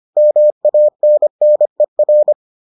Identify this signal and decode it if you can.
Morse code, heard as a keyed tone, about 25 wpm, 600 Hz. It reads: MANNER